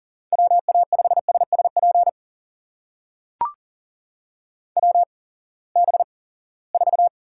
Morse code, heard as a keyed tone, about 40 wpm, spaced as for 23 wpm, 700 Hz. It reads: WA5SSP WB4
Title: USA Call Signs - 6 Characters - 8x Character Spacing - 40wpm